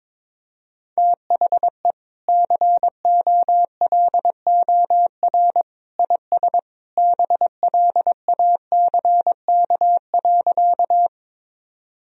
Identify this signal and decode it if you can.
Morse code, heard as a keyed tone, about 22 wpm, 700 Hz. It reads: THE COLOR IS BLACK.
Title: Sentences from Top 1000 Words - Review Part 2 - 22wpm